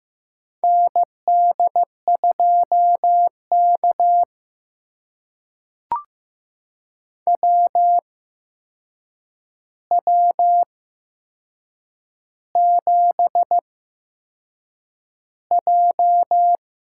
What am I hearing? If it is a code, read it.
Morse code, heard as a keyed tone, about 15 wpm, spaced as for 6 wpm, 700 Hz. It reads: ND2K WW7J
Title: USA Call Signs - 8x Character Spacing - 15wpm